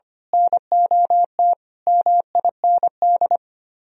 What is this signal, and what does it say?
Morse code, heard as a keyed tone, about 25 wpm, 700 Hz. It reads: NOT MIND
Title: Speed-Racing - Sets of 5 Words 25wpm